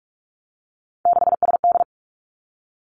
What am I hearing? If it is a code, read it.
Morse code, heard as a keyed tone, about 46 wpm, 700 Hz. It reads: 6SD
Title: Three Challenging Letter-Number Combinations 45wpm